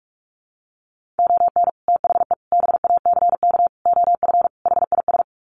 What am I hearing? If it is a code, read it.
Morse code, heard as a keyed tone, about 44 wpm, 700 Hz. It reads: ON THE BACK OF HIS
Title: Sets of 5 Words Encore 45wpm